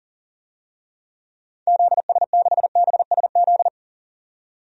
Morse code, heard as a keyed tone, about 40 wpm, 700 Hz. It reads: ZS6BSZ
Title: DXCC-100 Call Signs - 6 Characters - 40wpm